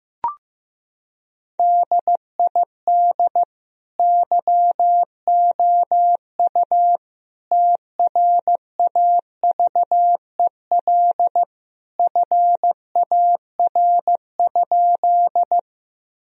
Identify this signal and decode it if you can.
Morse code, heard as a keyed tone, about 15 wpm, 700 Hz. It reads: DID YOU TRAVEL FAR?
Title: Sentences from Top 1000 Words - Review Part 2 - 15wpm